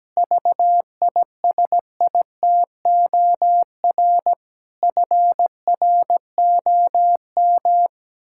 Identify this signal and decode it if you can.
Morse code, heard as a keyed tone, about 17 wpm, 700 Hz. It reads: VISITOR FROM
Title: Pangram Challenge v2 - 17wpm